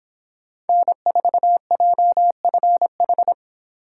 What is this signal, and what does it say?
Morse code, heard as a keyed tone, about 25 wpm, 700 Hz. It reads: N4JFH